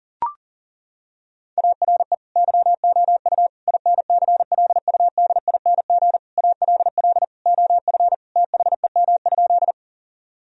Morse code, heard as a keyed tone, about 37 wpm, 700 Hz. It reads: ARE YOU INCLUDING ALL OF THEM?
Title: Sentences from Top 1000 Words - Review Part 4 - 40wpm